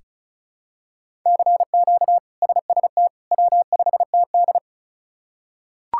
Morse code, heard as a keyed tone, about 36 wpm, 700 Hz. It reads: CQ SST W5TD
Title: K1USN SST QSO Practice - 35wpm